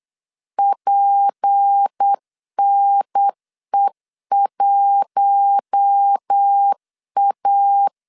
Morse code, heard as a keyed tone, about 8 wpm, 800 Hz. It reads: PNE1A
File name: LEZIONI / 6 PKX0 / 6 P 8wpm 3025